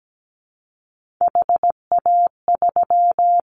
Morse code, heard as a keyed tone, about 17 wpm, 700 Hz. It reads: HA3
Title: Callsign Prefixes for Top 25 DXCC - Repeated 3x - 17wpm